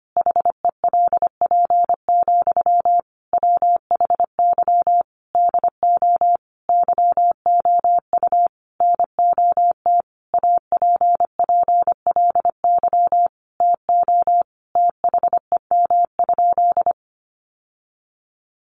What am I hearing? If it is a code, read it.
Morse code, heard as a keyed tone, about 25 wpm, 700 Hz. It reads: HELP, WHY DO YOU NOT APPLY TO THEM?
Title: Sentences from Top 1200 Words 25wpm